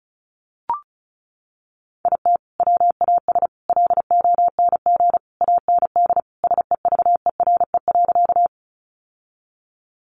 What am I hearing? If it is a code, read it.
Morse code, heard as a keyed tone, about 34 wpm, 700 Hz. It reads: IT WAS LONG AND SEVERE.
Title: Sentences from Top 1400 Words 35wpm